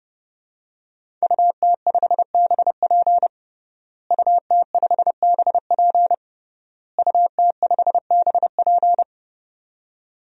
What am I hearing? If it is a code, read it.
Morse code, heard as a keyed tone, about 32 wpm, 700 Hz. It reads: UT5BP UT5BP UT5BP
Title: DXCC-100 Call Signs - 5 Characters - Repeated 3x - 30wpm